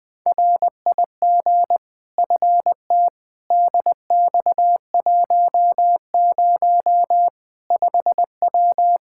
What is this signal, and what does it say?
Morse code, heard as a keyed tone, about 20 wpm, 700 Hz. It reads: RIG FT DX10 5W
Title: Practice QSO - Exchange 2 - Info - 20wpm